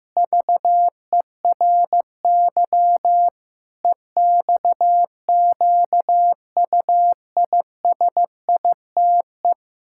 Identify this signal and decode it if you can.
Morse code, heard as a keyed tone, about 15 wpm, 700 Hz. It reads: VERY EXQUISITE
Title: Pangram Challenge v2 - 15wpm